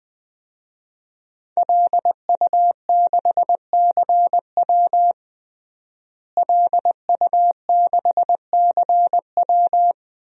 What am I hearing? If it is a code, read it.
Morse code, heard as a keyed tone, about 21 wpm, 700 Hz. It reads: LU6CW LU6CW
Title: DXCC-100 Call Signs - 5 Characters - Repeated 2x - 20wpm